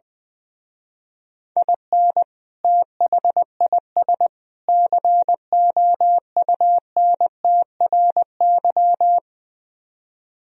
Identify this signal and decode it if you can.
Morse code, heard as a keyed tone, about 21 wpm, 700 Hz. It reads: IN THIS COUNTRY
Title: Speed-Racing - Sets of 3 Words 20wpm